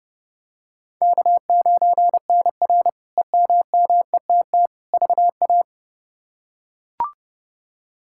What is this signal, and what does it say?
Morse code, heard as a keyed tone, about 31 wpm, 700 Hz. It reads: K9NR EMMETT VA
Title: K1USN SST QSO Practice - 30wpm